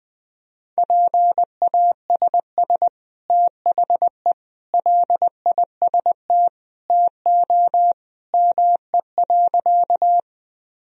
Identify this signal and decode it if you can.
Morse code, heard as a keyed tone, about 19 wpm, 700 Hz. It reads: PASS THE LIST TO ME.
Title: Sentences from Top 1000 Words - Review Part 2 - 20wpm